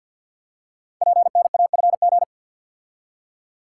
Morse code, heard as a keyed tone, about 53 wpm, 700 Hz. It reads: PNAFG
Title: Code Groups - 5 Letters - 50wpm